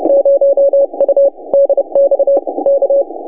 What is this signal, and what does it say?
Morse code, heard as a keyed tone, about 30 wpm, 560 Hz. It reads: 0U DX K
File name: vk6dxi 160m s50u